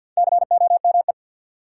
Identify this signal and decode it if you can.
Morse code, heard as a keyed tone, about 49 wpm, 700 Hz. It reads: COME
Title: Sentences from Top 100 Words - New Word Drill - 50wpm